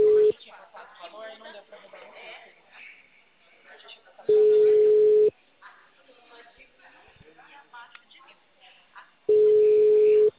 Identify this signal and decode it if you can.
Ringback tone (Europe, 425 Hz), with speech.